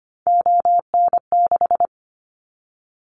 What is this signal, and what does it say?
Morse code, heard as a keyed tone, about 25 wpm, 700 Hz. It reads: ON6
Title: Callsign Prefixes for Top 50 DXCC - 25wpm